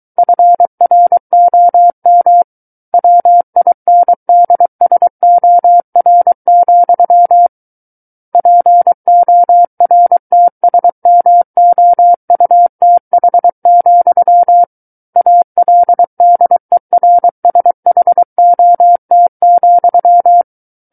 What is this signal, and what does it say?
Morse code, heard as a keyed tone, about 23 wpm, 700 Hz. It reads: FROM WINDSOR, PORTSMOUTH, ALDERSHOT,
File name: Book One / 14-Chapter 14 - 23 WPM